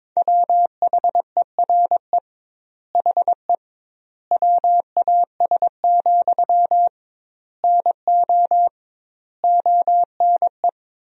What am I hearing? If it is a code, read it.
Morse code, heard as a keyed tone, about 22 wpm, 700 Hz. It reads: WHERE HE WAS, NO ONE
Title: 2x Word-Spacing - Sentences from Top 400 Words - 22wpm